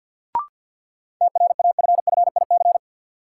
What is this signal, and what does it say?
Morse code, heard as a keyed tone, about 46 wpm, 700 Hz. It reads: TRAFFIC
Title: Sentences from Top 1300 Words - New Word Drill Alt - 50wpm